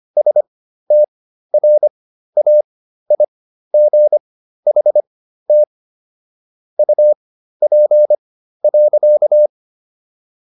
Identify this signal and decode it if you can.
Morse code, heard as a keyed tone, about 25 wpm, spaced as for 13 wpm, 600 Hz. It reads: STRAIGHT UP.